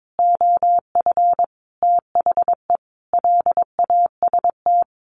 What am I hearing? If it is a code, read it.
Morse code, heard as a keyed tone, about 22 wpm, 700 Hz. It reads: OF THE LAST
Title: Sentences from Top 1500 Words 22wpm